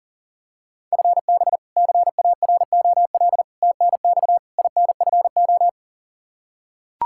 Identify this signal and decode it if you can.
Morse code, heard as a keyed tone, about 40 wpm, 700 Hz. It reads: FB CAROL TNX INFO
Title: Practice QSO - Exchange 3 - More Info - 40wpm